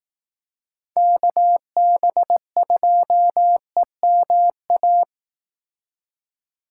Morse code, heard as a keyed tone, about 18 wpm, 700 Hz. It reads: KB2EMA
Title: Speed Racing - USA 2x3 Call Signs 15wpm